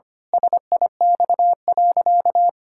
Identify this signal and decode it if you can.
Morse code, heard as a keyed tone, about 25 wpm, 700 Hz. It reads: SIX.
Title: Pangram Challenge v2 - 25wpm